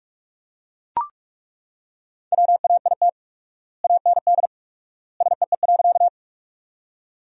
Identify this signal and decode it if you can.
Morse code, heard as a keyed tone, about 42 wpm, 690 Hz. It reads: WAIT AND SEE.
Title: 4x Word-Spacing - Sentences from Top 500 Words - 45wpm